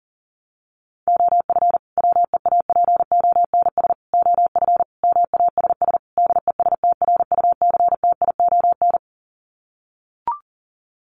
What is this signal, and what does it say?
Morse code, heard as a keyed tone, about 40 wpm, 700 Hz. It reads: OF WEAPONS OF MASS DESTRUCTION
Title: Sets of 5 Words Encore 40wpm